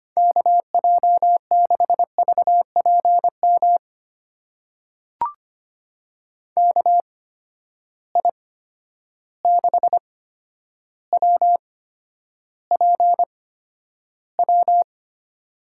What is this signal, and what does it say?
Morse code, heard as a keyed tone, about 25 wpm, spaced as for 11 wpm, 700 Hz. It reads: KJ6VPM KI6WPW
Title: USA Call Signs - 6 Characters - 8x Character Spacing - 25wpm